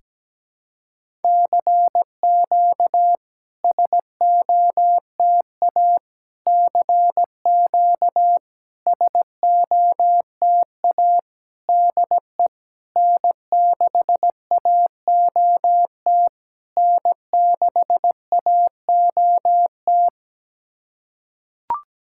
Morse code, heard as a keyed tone, about 17 wpm, 700 Hz. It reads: CQ SOTA CQ SOTA DE N6AOT N6AOT
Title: SOTA QSO Practice 17wpm